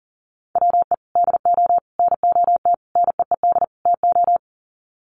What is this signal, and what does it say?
Morse code, heard as a keyed tone, about 40 wpm, 700 Hz. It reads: WE DO NOT NEED TO
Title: Sets of 5 Words Encore 40wpm